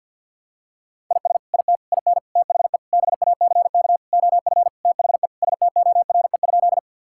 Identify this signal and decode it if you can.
Morse code, heard as a keyed tone, about 50 wpm, 700 Hz. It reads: IS IT IN THE BACK OF THE STORE?
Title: Sentences from Top 1000 Words - Review Part 5 - 50wpm